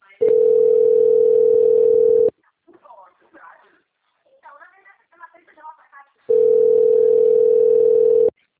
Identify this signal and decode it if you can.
Ringback tone (North America), 440 + 480 Hz, with speech.
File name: out- - - - - .19726